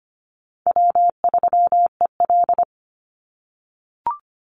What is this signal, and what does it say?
Morse code, heard as a keyed tone, about 25 wpm, 700 Hz. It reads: W3EL
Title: Easy Call Signs - 25wpm